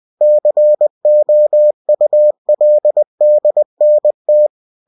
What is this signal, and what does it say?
Morse code, heard as a keyed tone, about 20 wpm, 600 Hz. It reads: COULDNT
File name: cw-books / winnie-the-pooh / winnie-20wpm-page-0103